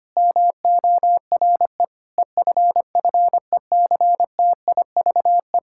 Morse code, heard as a keyed tone, about 25 wpm, 700 Hz. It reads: MORE EFFECTIVE
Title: Sentences from Top 1000 Words - Review Part 6 - 25wpm